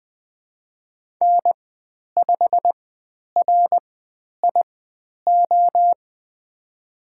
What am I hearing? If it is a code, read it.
Morse code, heard as a keyed tone, about 20 wpm, spaced as for 10 wpm, 700 Hz. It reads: N5RIO
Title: 109 Lesson 18g - Letter m - Call Signs 20wpm